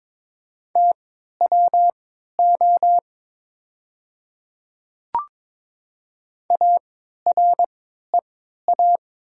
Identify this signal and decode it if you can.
Morse code, heard as a keyed tone, about 22 wpm, spaced as for 12 wpm, 700 Hz. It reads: TWO AREA
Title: 3x Character Spacing - Top 500 Words - 22wpm